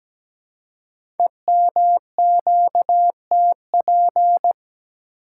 Morse code, heard as a keyed tone, about 17 wpm, 700 Hz. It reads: EMQTP
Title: Code Groups - 5 Letters - 17wpm